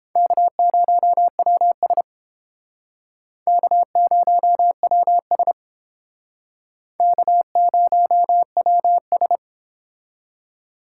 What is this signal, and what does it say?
Morse code, heard as a keyed tone, about 31 wpm, 700 Hz. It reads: K0WS K0WS K0WS